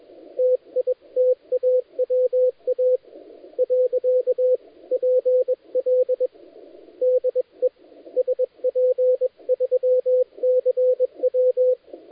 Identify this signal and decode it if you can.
Morse code, heard as a keyed tone, about 22 wpm, 510 Hz, with radio noise. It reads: TITAWA . PL DE SP3CW
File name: dlazaawansowanych / nagraniazpasma / titawaNETcontrol